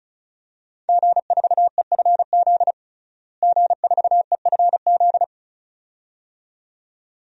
Morse code, heard as a keyed tone, about 36 wpm, 700 Hz. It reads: G4EFZ G4EFZ
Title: DXCC-100 Call Signs - 5 Characters - Repeated 2x - 35wpm